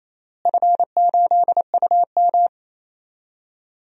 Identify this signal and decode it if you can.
Morse code, heard as a keyed tone, about 28 wpm, 700 Hz. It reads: F8UM
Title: DXCC-100 Call Signs - 4 Characters - Repeated 2x - 28wpm